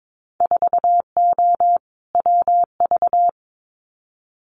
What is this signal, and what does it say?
Morse code, heard as a keyed tone, about 22 wpm, 700 Hz. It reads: 4O WV
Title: Winter Field Day Exchanges Encore 22wpm